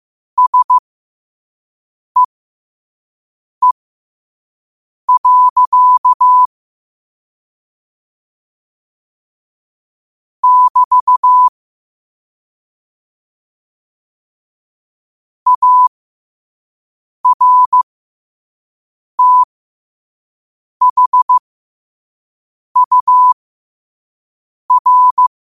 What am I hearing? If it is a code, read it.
Morse code, heard as a keyed tone, about 15 wpm, spaced as for 5 wpm, 1000 Hz. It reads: SEE. <BT> ARTHUR